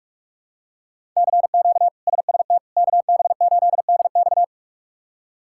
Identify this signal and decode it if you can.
Morse code, heard as a keyed tone, about 46 wpm, 700 Hz. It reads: CQ SST KB8DX
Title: K1USN SST QSO Practice - 45wpm